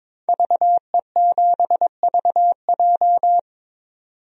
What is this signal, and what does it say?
Morse code, heard as a keyed tone, about 22 wpm, 700 Hz. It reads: VE7VJ
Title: DXCC-100 Call Signs - 5 Characters - 22wpm